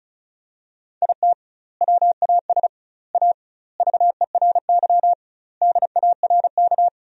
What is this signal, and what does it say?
Morse code, heard as a keyed tone, about 36 wpm, 700 Hz. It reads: IT WAS A VERY DARK
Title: 2x Word-Spacing - Sentences from Top 500 Words - 35wpm